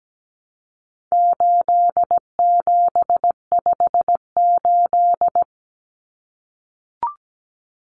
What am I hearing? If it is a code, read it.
Morse code, heard as a keyed tone, about 17 wpm, 700 Hz. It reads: 8758
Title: Numbers - 4-Digit 17wpm